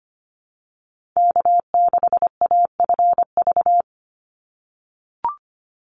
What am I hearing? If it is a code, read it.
Morse code, heard as a keyed tone, about 25 wpm, 700 Hz. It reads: K6AFV
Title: USA Call Signs 25wpm